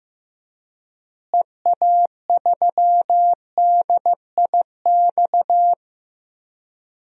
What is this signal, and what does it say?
Morse code, heard as a keyed tone, about 15 wpm, 700 Hz. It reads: EA3DIX
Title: DXCC-100 Call Signs - 6 Characters - Repeated 3x - 15wpm